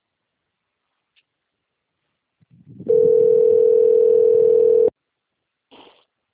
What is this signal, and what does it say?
Ringback tone (North America), 440 + 480 Hz.